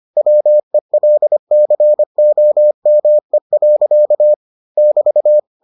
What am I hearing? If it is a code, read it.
Morse code, heard as a keyed tone, about 25 wpm, 600 Hz. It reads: WELCOME. <BT>